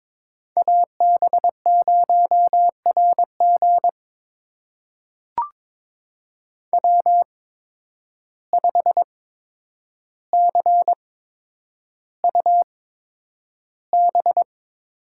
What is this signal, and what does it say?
Morse code, heard as a keyed tone, about 22 wpm, spaced as for 9 wpm, 700 Hz. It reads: AB0RG W5CUB
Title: USA Call Signs - 5 Characters - 8x Character Spacing - 22wpm